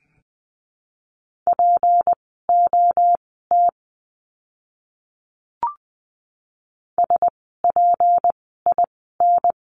Morse code, heard as a keyed tone, about 20 wpm, spaced as for 15 wpm, 700 Hz. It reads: POT SPIN
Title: 153 Lesson 25f - Letter p - Words 20wpm